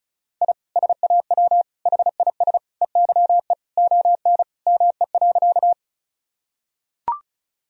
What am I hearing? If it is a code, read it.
Morse code, heard as a keyed tone, about 36 wpm, 700 Hz. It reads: I SAW HIS EYE ON ME.
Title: Sentences from Top 1000 Words - Review Part 1 - 35wpm